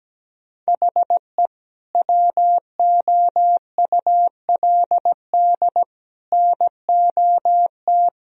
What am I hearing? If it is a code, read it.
Morse code, heard as a keyed tone, about 17 wpm, 700 Hz. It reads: HE WOULD NOT
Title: Sentences from Top 1000 Words - Review Part 1 - 17wpm